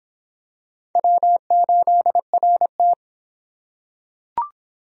Morse code, heard as a keyed tone, about 24 wpm, 700 Hz. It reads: W8RT